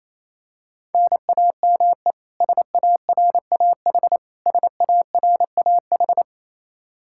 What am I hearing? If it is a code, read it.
Morse code, heard as a keyed tone, about 28 wpm, 700 Hz. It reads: NAME SARAH SARAH
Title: Practice QSO - Exchange 1 - Beginning - 28wpm